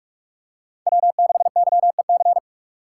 Morse code, heard as a keyed tone, about 44 wpm, 700 Hz. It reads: W6YEC